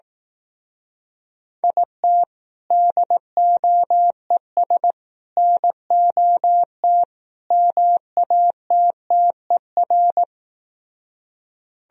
Morse code, heard as a keyed tone, about 18 wpm, 700 Hz. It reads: IT DOES NOT MATTER
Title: Speed-Racing - Sets of 4 Words 15wpm